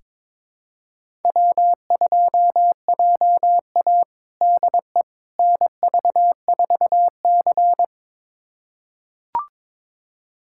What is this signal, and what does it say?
Morse code, heard as a keyed tone, about 22 wpm, 700 Hz. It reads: W2JA DE NV4C
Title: USA Callsign DE Callsign - 4 Characters - 22wpm